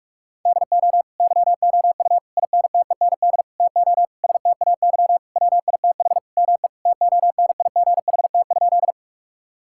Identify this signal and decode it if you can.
Morse code, heard as a keyed tone, about 46 wpm, 700 Hz. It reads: DO YOU INTEND TO STAY WITH ME TONIGHT?
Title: Sentences from Top 1600 Words 45wpm